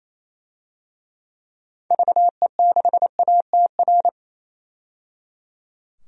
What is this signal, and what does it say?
Morse code, heard as a keyed tone, about 28 wpm, 700 Hz. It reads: VE6ATR
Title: DXCC-100 Call Signs - 6 Characters - 28wpm